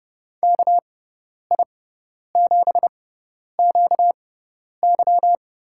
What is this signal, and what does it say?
Morse code, heard as a keyed tone, about 32 wpm, spaced as for 10 wpm, 700 Hz. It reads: KI7QY
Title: USA Call Signs - 6 Characters - 6x Character Spacing - 30wpm